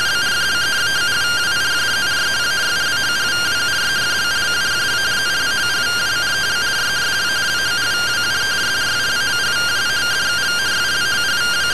Two-tone teleprinter signal, about 170 Hz shift.